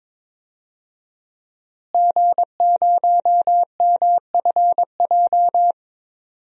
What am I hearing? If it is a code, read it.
Morse code, heard as a keyed tone, about 22 wpm, 700 Hz. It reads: G0MFJ
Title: DXCC-100 Call Signs - 5 Characters - Repeated 2x - 22wpm